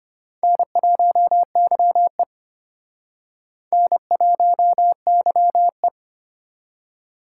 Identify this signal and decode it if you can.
Morse code, heard as a keyed tone, about 29 wpm, 700 Hz. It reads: N1YE N1YE
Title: Speed Racing - USA 1x2 Call Signs 25wpm